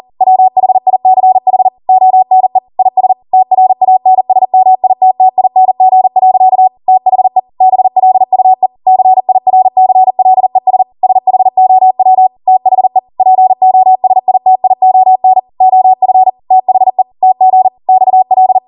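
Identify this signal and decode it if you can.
Morse code, heard as a keyed tone, about 40 wpm, 750 Hz. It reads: WHICH ONE IS TRANSMITTING. THE BLUE CIRCLES SHOW THE POSITION OF THE TG XL
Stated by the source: ARRL 40 WPM Morse code practice transmission.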